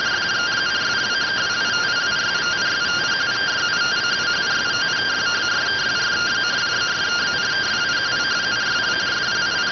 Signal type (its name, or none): two-tone teleprinter signal